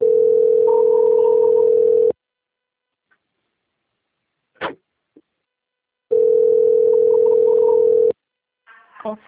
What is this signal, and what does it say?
Ringback tone (North America), 440 + 480 Hz.